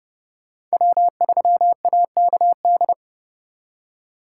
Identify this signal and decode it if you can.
Morse code, heard as a keyed tone, about 36 wpm, 700 Hz. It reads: W3AKD
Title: Speed Racing - USA 1x3 Call Signs 30wpm